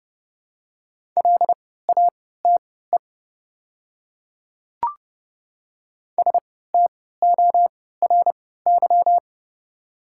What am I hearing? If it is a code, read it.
Morse code, heard as a keyed tone, about 30 wpm, spaced as for 17 wpm, 700 Hz. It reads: LATE STORY
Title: 3x Character Spacing - Top 300 Words - 30wpm